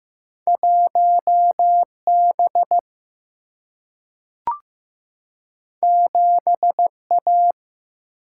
Morse code, heard as a keyed tone, about 15 wpm, 700 Hz. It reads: 1 B 7 A
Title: Single-Letter Single-Number 15wpm